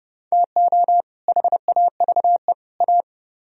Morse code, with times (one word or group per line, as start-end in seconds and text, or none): 0.3-1.0 TO
1.3-2.5 HAVE
2.8-3.0 A